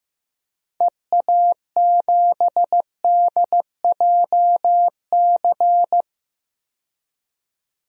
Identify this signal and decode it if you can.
Morse code, heard as a keyed tone, about 15 wpm, 700 Hz. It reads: EA7DJC
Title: DXCC-100 Call Signs - 6 Characters - 15wpm